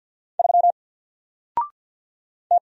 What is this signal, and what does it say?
Morse code, heard as a keyed tone, about 53 wpm, 700 Hz. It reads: 3 T
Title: Single Letter-Number Alt 50wpm